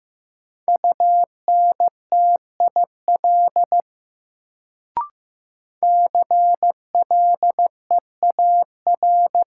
Morse code, heard as a keyed tone, about 15 wpm, 700 Hz. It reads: UNTIL CLEAR
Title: Sentences from Top 500 Words - New Word Drill - 15wpm